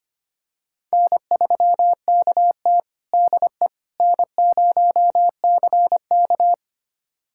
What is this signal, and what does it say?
Morse code, heard as a keyed tone, about 25 wpm, 700 Hz. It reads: N3KT DE N0CK
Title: USA Callsign DE Callsign - 4 Characters - 25wpm